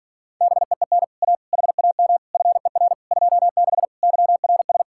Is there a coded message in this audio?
Yes: Morse code from 0.4 to 4.8 s, heard as a keyed tone, about 51 wpm, 700 Hz. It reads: BEEN A HAM FER 26 YRS